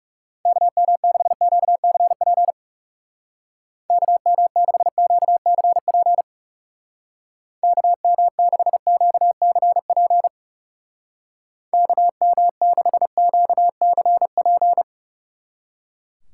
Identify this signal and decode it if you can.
Morse code, heard as a keyed tone, about 38 wpm, 700 Hz. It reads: KM6QCP KM6QCP KM6QCP KM6QCP